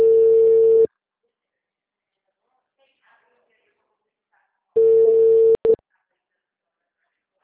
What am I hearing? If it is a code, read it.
Ringback tone (China, 450 Hz), with speech.